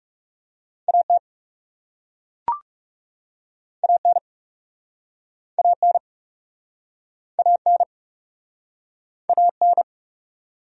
Morse code, heard as a keyed tone, about 42 wpm, 700 Hz. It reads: AT AN AN AN AN